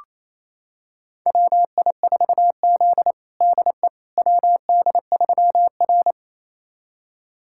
Morse code, heard as a keyed tone, about 28 wpm, 700 Hz. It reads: WI4Z DE WD3R
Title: USA Callsign DE Callsign - 4 Characters - 28wpm